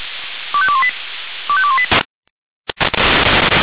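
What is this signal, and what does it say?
Five-tone selective call (ZVEI), heard as quick runs of five tones, tones of about 80 ms. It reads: 25218 25218
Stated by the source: ZVEI five-tone selective call received on SDR.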